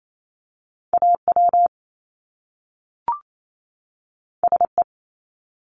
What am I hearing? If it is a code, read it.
Morse code, heard as a keyed tone, about 28 wpm, 700 Hz. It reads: AW SE
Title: Top Two Letter Combinations in Words 28wpm